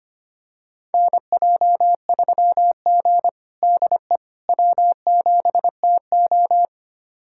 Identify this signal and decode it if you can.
Morse code, heard as a keyed tone, about 25 wpm, 700 Hz. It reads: NJ3G DE W7TO
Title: USA Callsign DE Callsign - 4 Characters - 25wpm